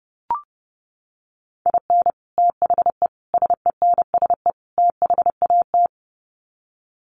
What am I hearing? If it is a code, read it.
Morse code, heard as a keyed tone, about 30 wpm, 700 Hz. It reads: IN THE SENSE THAT
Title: Sets of 4 Words 30wpm